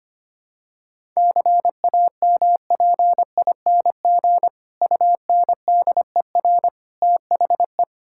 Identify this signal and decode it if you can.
Morse code, heard as a keyed tone, about 25 wpm, 700 Hz. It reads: CAMPING UNDER THE
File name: practice / 25wpm / Difficult Sentences with Hints - 25wpm